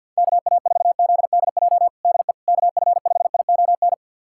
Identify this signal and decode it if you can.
Morse code, heard as a keyed tone, about 52 wpm, 700 Hz. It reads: KA4ZDJ DE KF5ION
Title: USA Callsign DE Callsign - 6 Characters - 50wpm